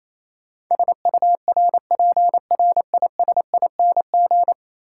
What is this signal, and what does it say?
Morse code, heard as a keyed tone, about 28 wpm, 700 Hz. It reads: SURPRISING
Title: Sets of 5 Words - New Word Drill - 28wpm